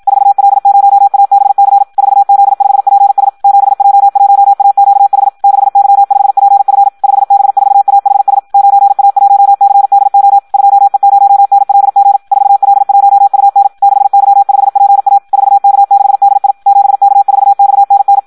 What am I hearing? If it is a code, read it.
Morse code, heard as a keyed tone, about 56 wpm, 800 Hz. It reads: 4L0ABC 4Z5PS 7J1AQH 6Y5PF 5B4AHS 9A1CDO 2E0NLG 4L1UN 6Y5PA 4K6DI 7K4XNN
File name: Audio practica CW / qrq-tx-54 2